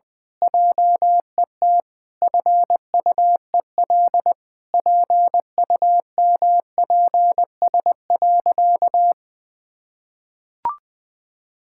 Morse code, heard as a keyed tone, about 20 wpm, 700 Hz. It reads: JET FUEL PUMPS.
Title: Pangram Challenge v2 - 20wpm